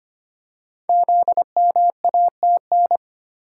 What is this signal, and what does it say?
Morse code, heard as a keyed tone, about 25 wpm, 700 Hz. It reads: ZMATN
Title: Code Groups - 5 Letters - 25wpm